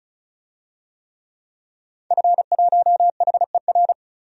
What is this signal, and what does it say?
Morse code, heard as a keyed tone, about 36 wpm, 700 Hz. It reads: F1HER
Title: DXCC-100 Call Signs - 5 Characters - 35wpm